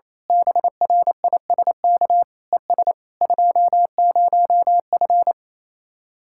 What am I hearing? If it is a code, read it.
Morse code, heard as a keyed tone, about 27 wpm, 700 Hz. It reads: BRISK ES 20F
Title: Speed-Racing Common QSO Phrases Encore - 28wpm